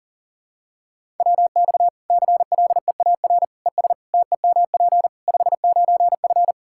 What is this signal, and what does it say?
Morse code, heard as a keyed tone, about 40 wpm, 700 Hz. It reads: WX CLEAR ES TEMP 59F
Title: K1USN SST QSO Practice 40wpm